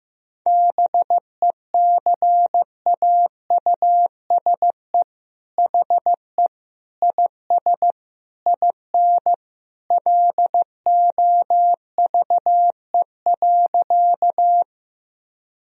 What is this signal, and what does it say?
Morse code, heard as a keyed tone, about 16 wpm, 700 Hz. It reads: BECAUSE HE IS IN LOVE.
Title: Sentences from Top 1000 Words - Review Part 3 - 15wpm